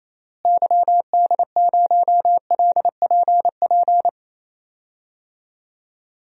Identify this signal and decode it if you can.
Morse code, heard as a keyed tone, about 28 wpm, 700 Hz. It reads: YD0LPP
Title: DXCC-100 Call Signs - 6 Characters - Repeated 2x - 28wpm